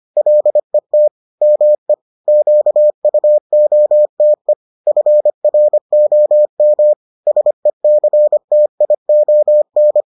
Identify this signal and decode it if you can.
Morse code, heard as a keyed tone, about 25 wpm, 600 Hz. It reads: LET ME QUOTE FROM SECTION